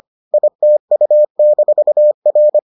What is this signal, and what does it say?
Morse code, heard as a keyed tone, about 25 wpm, 600 Hz. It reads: ITU-R